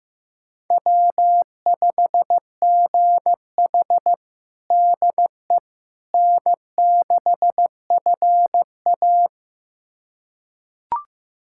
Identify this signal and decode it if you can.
Morse code, heard as a keyed tone, about 15 wpm, 700 Hz. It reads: W5GH DE N6FA
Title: USA Callsign DE Callsign - 4 Characters - 15wpm